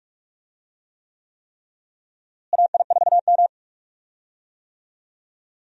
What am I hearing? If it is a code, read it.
Morse code, heard as a keyed tone, about 46 wpm, 700 Hz. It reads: AI4M
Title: USA Call Signs - 4 Characters - Repeated 3x - 45wpm